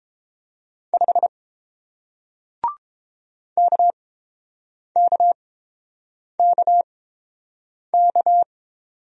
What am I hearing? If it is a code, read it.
Morse code, heard as a keyed tone, about 30 wpm, spaced as for 7 wpm, 700 Hz. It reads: 5 KKKK